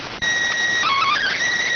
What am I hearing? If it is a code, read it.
SSTV (slow-scan television), PD 290.